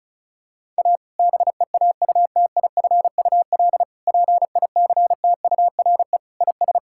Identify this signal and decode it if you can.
Morse code, heard as a keyed tone, about 36 wpm, 700 Hz. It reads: A BEAUTIFUL PICTURE IS
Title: Sentences from Top 1000 Words - Review Part 6 - 35wpm